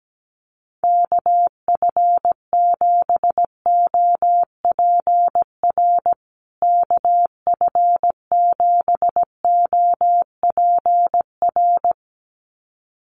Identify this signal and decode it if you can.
Morse code, heard as a keyed tone, about 17 wpm, 700 Hz. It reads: KF7OPR KF7OPR
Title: USA Call Signs Repeated 2x 17wpm